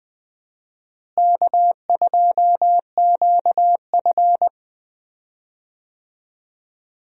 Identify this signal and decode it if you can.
Morse code, heard as a keyed tone, about 20 wpm, 700 Hz. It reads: K2QF
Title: USA Call Signs - 4 Characters - Repeated 2x - 20wpm